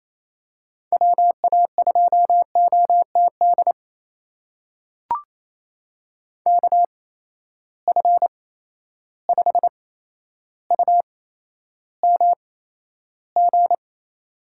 Morse code, heard as a keyed tone, about 28 wpm, spaced as for 12 wpm, 700 Hz. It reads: WA2OTD KF5UMG
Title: USA Call Signs - 6 Characters - 8x Character Spacing - 28wpm